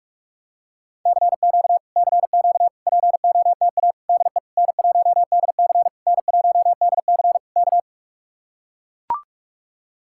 Morse code, heard as a keyed tone, about 44 wpm, 700 Hz. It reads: CQ CQ POTA DE N1DC N1DC K